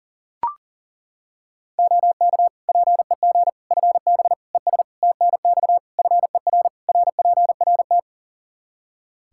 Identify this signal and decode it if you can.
Morse code, heard as a keyed tone, about 40 wpm, 700 Hz. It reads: OK PEG FB ES TNX FER RPRT
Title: Practice QSO - Exchange 2 - Info - 40wpm